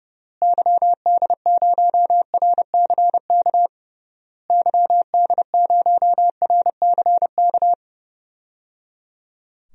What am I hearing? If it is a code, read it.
Morse code, heard as a keyed tone, about 30 wpm, 700 Hz. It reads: YD0RCK YD0RCK
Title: DXCC-100 Call Signs - 6 Characters - Repeated 3x - 30wpm